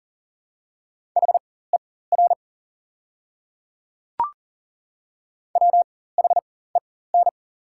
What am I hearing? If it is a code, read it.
Morse code, heard as a keyed tone, about 40 wpm, spaced as for 19 wpm, 690 Hz. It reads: HER WHEN
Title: 4x Character Spacing - Top 200 Words - 40wpm